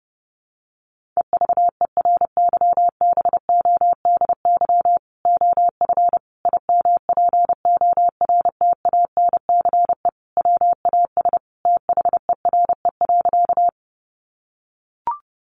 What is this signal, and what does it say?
Morse code, heard as a keyed tone, about 30 wpm, 700 Hz. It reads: EVERYBODY OF IMPORTANCE WAS THERE.
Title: Sentences from Top 1400 Words 30wpm